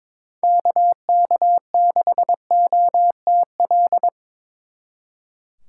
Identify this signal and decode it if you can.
Morse code, heard as a keyed tone, about 25 wpm, 700 Hz. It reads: KK6OTL
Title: Speed Racing - USA 2x3 Call Signs 22wpm